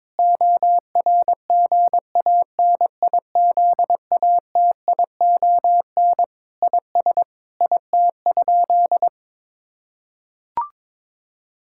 Morse code, heard as a keyed tone, about 22 wpm, 700 Hz. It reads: ORGANIZATION IS IT?
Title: Sentences from Top 1000 Words - Review Part 5 - 22wpm